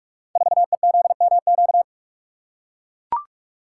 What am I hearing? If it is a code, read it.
Morse code, heard as a keyed tone, about 44 wpm, 700 Hz. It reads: 4EZMQ
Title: Code Groups - 5 Letter:Numbers - 45wpm